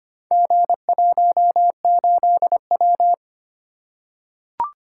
Morse code, heard as a keyed tone, about 25 wpm, 700 Hz. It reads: G18W
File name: practice / 25wpm / Four Letter-Number Combinations - 25wpm